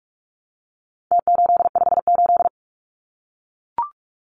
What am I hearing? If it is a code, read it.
Morse code, heard as a keyed tone, about 44 wpm, 700 Hz. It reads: T858
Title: Numbers - Contest Style - 4-Digit 45wpm